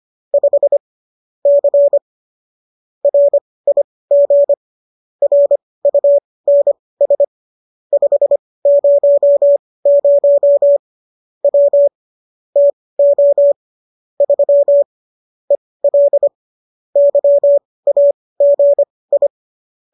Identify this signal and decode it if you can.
Morse code, heard as a keyed tone, about 25 wpm, spaced as for 18 wpm, 570 Hz. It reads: 5 C RIG RUNS 500 W TO 3 EL YAGI
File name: qso203 18